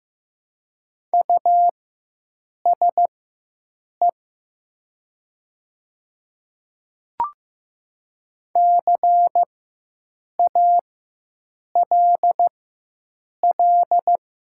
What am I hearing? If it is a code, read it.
Morse code, heard as a keyed tone, about 15 wpm, spaced as for 7 wpm, 700 Hz. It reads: USE CALL
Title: 4x Character Spacing - Top 100 Words - 15wpm